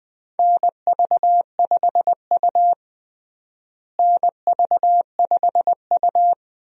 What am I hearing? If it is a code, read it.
Morse code, heard as a keyed tone, about 21 wpm, 700 Hz. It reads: NV5U NV5U
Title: USA Call Signs - 4 Characters - Repeated 3x - 20wpm